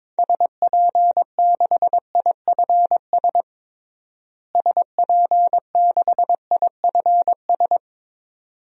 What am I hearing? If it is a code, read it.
Morse code, heard as a keyed tone, about 22 wpm, 700 Hz. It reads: SP6IFS SP6IFS
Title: DXCC-100 Call Signs - 6 Characters - Repeated 2x - 22wpm